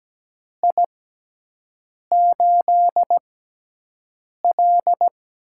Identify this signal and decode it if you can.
Morse code, heard as a keyed tone, about 17 wpm, spaced as for 6 wpm, 700 Hz. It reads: I8L
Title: USA Call Signs - 4 Characters - 6x Character Spacing - 17wpm